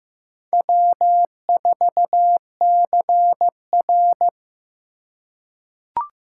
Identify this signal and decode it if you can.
Morse code, heard as a keyed tone, about 15 wpm, 700 Hz. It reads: W4CR
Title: USA Call Signs - 2x Character Spacing - 15wpm